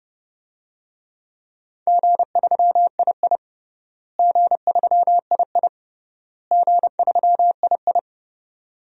Morse code, heard as a keyed tone, about 30 wpm, 700 Hz. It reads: G3II G3II G3II
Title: DXCC-100 Call Signs - 4 Characters - Repeated 3x - 30wpm